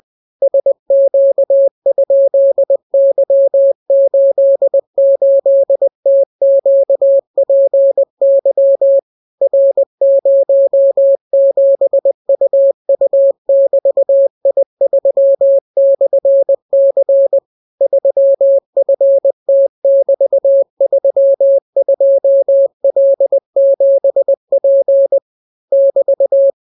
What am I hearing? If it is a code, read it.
Morse code, heard as a keyed tone, about 20 wpm, 550 Hz. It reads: SQ?Y88TQPY R07UU<BT>I3/C 3FT<BT>32L7P <BT>